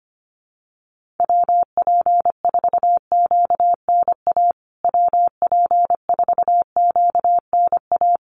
Morse code, heard as a keyed tone, about 25 wpm, 700 Hz. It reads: WP4QNA WP4QNA
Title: USA Call Signs Repeated 3x 25wpm